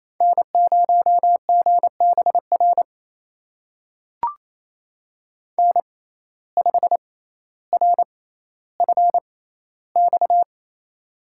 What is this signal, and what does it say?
Morse code, heard as a keyed tone, about 27 wpm, spaced as for 14 wpm, 700 Hz. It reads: N0GBR N5RFX
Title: USA Call Signs - 5 Characters - 6x Character Spacing - 28wpm